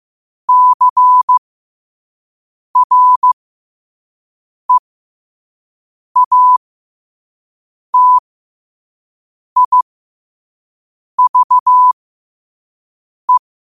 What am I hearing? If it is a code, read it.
Morse code, heard as a keyed tone, about 15 wpm, spaced as for 5 wpm, 1000 Hz. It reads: CREATIVE
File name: qotdfiles / 5wpm